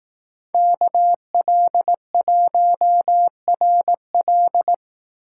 Morse code, heard as a keyed tone, about 18 wpm, 700 Hz. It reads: KL1RL
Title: Speed Racing - USA Call Signs 15wpm